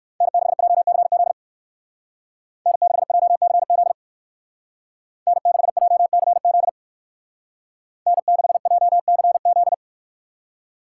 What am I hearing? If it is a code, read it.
Morse code, heard as a keyed tone, about 64 wpm, 700 Hz. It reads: N6JCZ N6JCZ N6JCZ N6JCZ